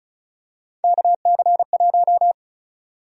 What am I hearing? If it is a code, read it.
Morse code, heard as a keyed tone, about 34 wpm, 700 Hz. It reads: KC1
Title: USA Call Sign Prefixes - 35wpm